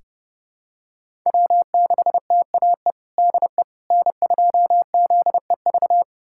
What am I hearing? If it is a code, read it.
Morse code, heard as a keyed tone, about 30 wpm, 700 Hz. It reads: W6TAE DE N2ZEV
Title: USA Callsign DE Callsign - 5 Characters - 30wpm